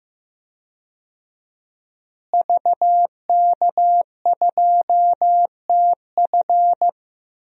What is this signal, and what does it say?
Morse code, heard as a keyed tone, about 15 wpm, 700 Hz. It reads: VK2TF
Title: DXCC-100 Call Signs - 5 Characters - Repeated 3x - 15wpm